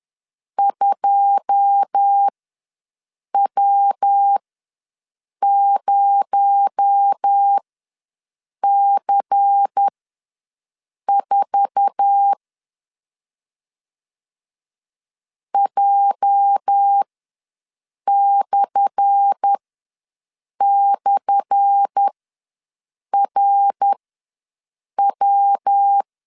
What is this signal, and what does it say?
Morse code, heard as a keyed tone, about 10 wpm, spaced as for 6 wpm, 800 Hz. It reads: 2W0C4 J//RW
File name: LEZIONI / 10 SIGNS / 10A SLASH ws 10wpm 4121